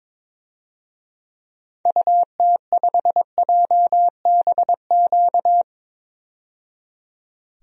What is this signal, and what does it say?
Morse code, heard as a keyed tone, about 22 wpm, 700 Hz. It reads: UT5JBQ
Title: DXCC-100 Call Signs - 6 Characters - 22wpm